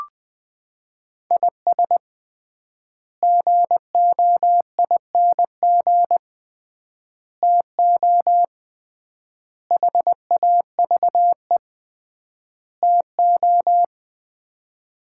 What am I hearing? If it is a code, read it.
Morse code, heard as a keyed tone, about 20 wpm, 700 Hz. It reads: IS GOING TO HAVE TO
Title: 3x Word-Spacing - Sets of 5 Words - 20wpm